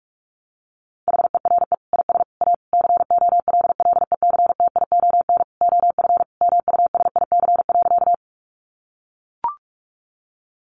Morse code, heard as a keyed tone, about 44 wpm, 700 Hz. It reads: HERE IS A COLLECTION OF MUSIC.
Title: Sentences from Top 1200 Words 45wpm